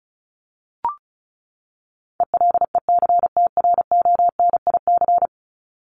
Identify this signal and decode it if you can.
Morse code, heard as a keyed tone, about 34 wpm, 700 Hz. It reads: ELECTRONIC
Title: Top 1001-1250 Words 35wpm